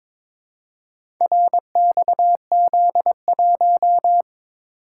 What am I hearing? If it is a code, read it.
Morse code, heard as a keyed tone, about 22 wpm, 700 Hz. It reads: RXZ1
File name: practice / 22wpm / Four Letter-Number Combinations - 22wpm